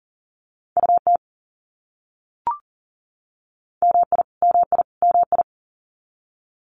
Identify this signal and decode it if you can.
Morse code, heard as a keyed tone, about 40 wpm, 700 Hz. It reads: UT MI MI MI
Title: US States Repeated 3x 40wpm